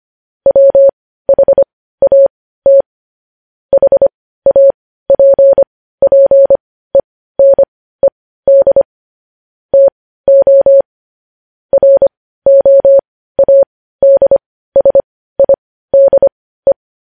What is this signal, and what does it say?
Morse code, heard as a keyed tone, about 25 wpm, spaced as for 15 wpm, 550 Hz. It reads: WHAT HAPPENED TO ROADSIDE
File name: podcast / News headlines-1x-25x15